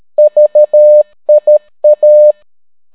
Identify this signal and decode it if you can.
Morse code, heard as a keyed tone, about 13 wpm, 600 Hz. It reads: VIA